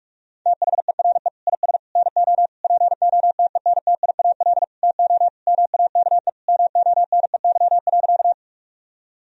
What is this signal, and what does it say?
Morse code, heard as a keyed tone, about 46 wpm, 700 Hz. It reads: THERE IS NO POTENTIAL TO MAKE MONEY.
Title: Sentences from Top 1000 Words - Review Part 4 - 45wpm